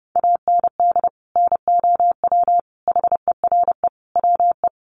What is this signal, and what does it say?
Morse code, heard as a keyed tone, about 30 wpm, 700 Hz. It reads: AND NOW HERE WE
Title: Sentences from Top 1400 Words 30wpm